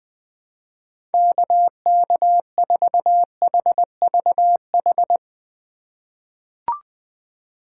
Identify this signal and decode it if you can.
Morse code, heard as a keyed tone, about 20 wpm, 700 Hz. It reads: KK4HVH
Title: USA Call Signs - 4x Character Spacing - 20wpm